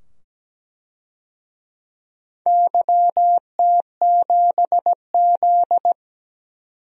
Morse code, heard as a keyed tone, about 17 wpm, 700 Hz. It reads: YT7Z
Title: DXCC-100 Call Signs - 4 Characters - 17wpm